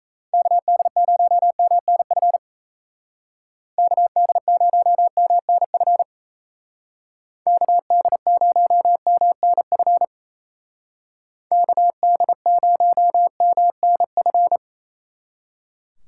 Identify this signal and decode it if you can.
Morse code, heard as a keyed tone, about 33 wpm, 700 Hz. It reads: KD0MNF KD0MNF KD0MNF KD0MNF
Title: Speed Racing - USA Call Signs 28wpm